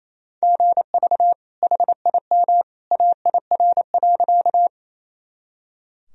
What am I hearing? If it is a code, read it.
Morse code, heard as a keyed tone, about 28 wpm, 700 Hz. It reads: GV HIM AIR.
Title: Sentences from Top 200 Words and Abbreviations 28wpm